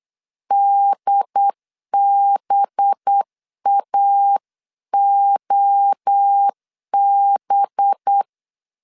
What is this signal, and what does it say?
Morse code, heard as a keyed tone, about 8 wpm, 800 Hz. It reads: DBAOB
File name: 8wpm